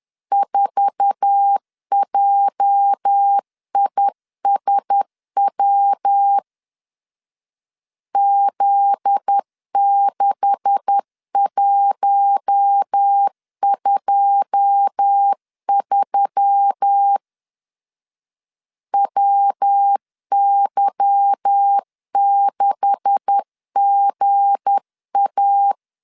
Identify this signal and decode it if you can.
Morse code, heard as a keyed tone, about 10 wpm, 800 Hz. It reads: 4JISW Z6123 WY6GA
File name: LEZIONI / 9 286 10wpm 2735